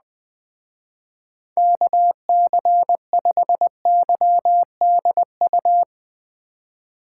Morse code, heard as a keyed tone, about 20 wpm, 700 Hz. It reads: KC5YDU